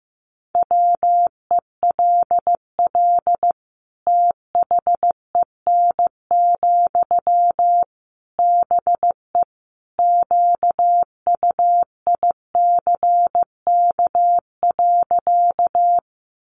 Morse code, heard as a keyed tone, about 15 wpm, 700 Hz. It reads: WELL THEN, BE QUICK.
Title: Sentences from Top 500 Words 15wpm